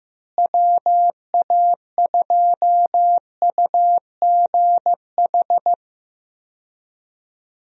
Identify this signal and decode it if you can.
Morse code, heard as a keyed tone, about 15 wpm, 700 Hz. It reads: WA2UGH
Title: USA Call Signs - 6 Characters - Repeated 2x - 15wpm